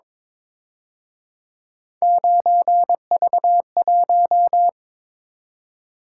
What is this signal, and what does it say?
Morse code, heard as a keyed tone, about 26 wpm, 700 Hz. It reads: 9V1
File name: Speed-Racing - Three Letter-Number Combinations - 22wpm